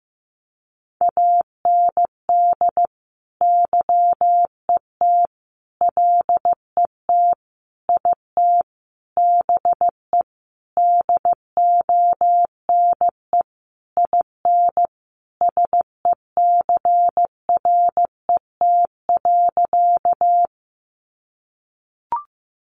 Morse code, heard as a keyed tone, about 15 wpm, 700 Hz. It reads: AND YET LET IT BE DONE IN SECRET.
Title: Sentences from Top 1200 Words 15wpm